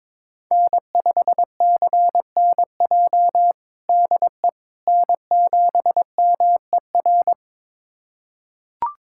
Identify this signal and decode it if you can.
Morse code, heard as a keyed tone, about 22 wpm, 700 Hz. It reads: N5CNJ DE N7MER
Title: USA Callsign DE Callsign - 5 Characters - 22wpm